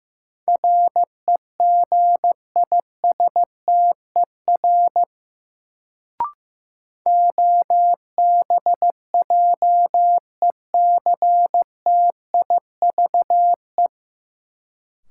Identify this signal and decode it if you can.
Morse code, heard as a keyed tone, about 15 wpm, 700 Hz. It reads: REGISTER OBJECTIVE
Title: Sentences from Top 1200 Words - New Word Drill - 15wpm